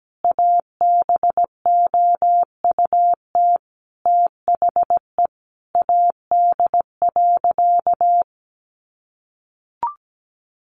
Morse code, heard as a keyed tone, about 17 wpm, 700 Hz. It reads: ABOUT THE AD.
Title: Sentences from Top 1300 Words 17wpm